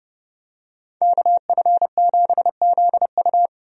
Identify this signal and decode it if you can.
Morse code, heard as a keyed tone, about 30 wpm, 700 Hz. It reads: KF7ZU
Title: USA Call Signs - 5 Characters - 30wpm